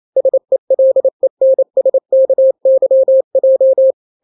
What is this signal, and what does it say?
Morse code, heard as a keyed tone, about 28 wpm, 530 Hz. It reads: SELENSKYJ
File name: daily / 27wpm